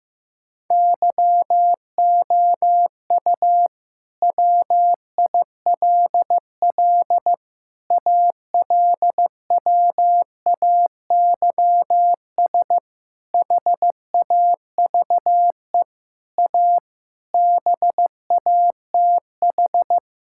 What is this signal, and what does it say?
Morse code, heard as a keyed tone, about 16 wpm, 700 Hz. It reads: YOU WILL ALWAYS HAVE A BATH
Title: Sentences from Top 1600 Words 15wpm